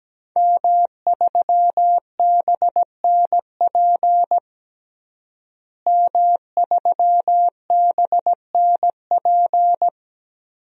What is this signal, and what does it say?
Morse code, heard as a keyed tone, about 17 wpm, 700 Hz. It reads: M3BNP M3BNP
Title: DXCC-100 Call Signs - 5 Characters - Repeated 3x - 17wpm